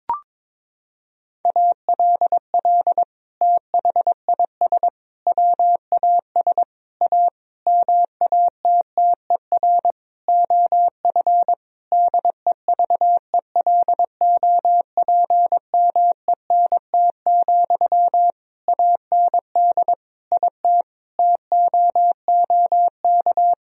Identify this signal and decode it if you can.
Morse code, heard as a keyed tone, about 22 wpm, 700 Hz. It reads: ALL THIS WAS A MATTER OF DEVELOPMENT, AND IT TOOK
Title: Sentences from Top 1000 Words - Review Part 4 - 22wpm